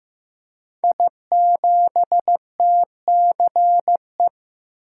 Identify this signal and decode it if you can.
Morse code, heard as a keyed tone, about 15 wpm, 700 Hz. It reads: I7TCE
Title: Code Groups - 5 Letter:Numbers - 15wpm